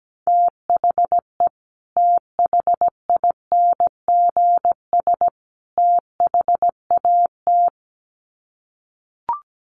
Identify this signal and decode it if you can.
Morse code, heard as a keyed tone, about 17 wpm, 700 Hz. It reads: THE THINGS THAT
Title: Sets of 3 Words 17wpm